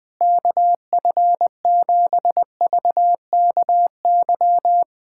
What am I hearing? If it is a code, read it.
Morse code, heard as a keyed tone, about 21 wpm, 700 Hz. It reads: KF7VKY
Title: USA Callsign DE Callsign 20wpm